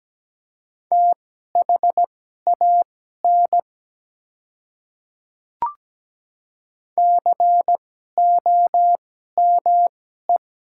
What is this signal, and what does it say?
Morse code, heard as a keyed tone, about 17 wpm, spaced as for 12 wpm, 700 Hz. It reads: THAN COME
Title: 2x Character Spacing - Top 100 Words - 17wpm